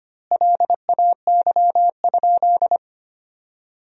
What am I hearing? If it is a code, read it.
Morse code, heard as a keyed tone, about 25 wpm, 700 Hz. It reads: LAY?
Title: 4x Word-Spacing - Sentences from Top 400 Words - 25wpm